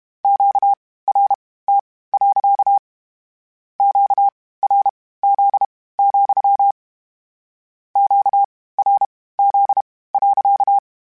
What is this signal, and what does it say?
Morse code, heard as a keyed tone, about 32 wpm, spaced as for 18 wpm, 800 Hz. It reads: QRT. QRZ, QRZ.
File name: Q Calls / Q Calls-mp3s / Q Calls sppc @30